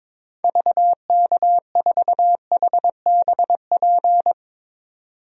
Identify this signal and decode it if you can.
Morse code, heard as a keyed tone, about 22 wpm, 700 Hz. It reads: VK4HBP
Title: DXCC-100 Call Signs - 6 Characters - 22wpm